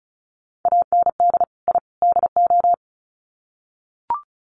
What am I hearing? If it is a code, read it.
Morse code, heard as a keyed tone, about 34 wpm, 700 Hz. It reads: AND I DO
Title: Sets of 3 Words 35wpm